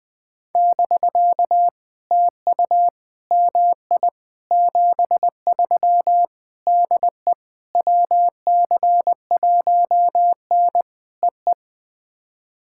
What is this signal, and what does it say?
Morse code, heard as a keyed tone, about 20 wpm, 700 Hz. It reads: <BK> TU MI 73 DE WC1N EE
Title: POTA QSO Practice - 20wpm